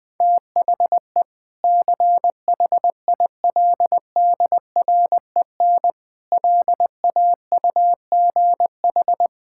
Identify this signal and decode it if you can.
Morse code, heard as a keyed tone, about 20 wpm, 700 Hz. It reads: THE CHILDREN LAUGH
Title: Sentences with 2-Word Phrases - 20wpm